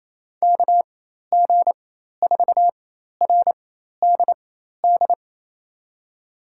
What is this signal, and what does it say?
Morse code, heard as a keyed tone, about 27 wpm, spaced as for 13 wpm, 700 Hz. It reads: KG4RDD
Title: USA Call Signs - 6 Characters - 4x Character Spacing - 28wpm